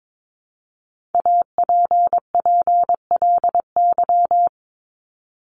Morse code, heard as a keyed tone, about 22 wpm, 700 Hz. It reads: APPLY
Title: ED Suffix - Top 100 Words with Base Words 22wpm